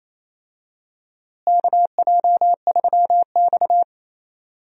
Morse code, heard as a keyed tone, about 28 wpm, 700 Hz. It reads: KJ3X
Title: USA Call Signs - 4 Characters - Repeated 2x - 28wpm